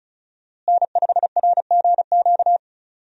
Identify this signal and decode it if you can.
Morse code, heard as a keyed tone, about 36 wpm, 700 Hz. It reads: N5RGQ
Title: USA Call Signs - 5 Characters - Repeated 3x - 35wpm